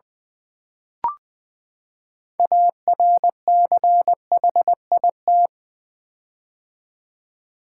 Morse code, heard as a keyed tone, about 20 wpm, 700 Hz. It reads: ARCHIT
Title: Build A Word - 6 Letters At A Time - 12-Letter Words - 20wpm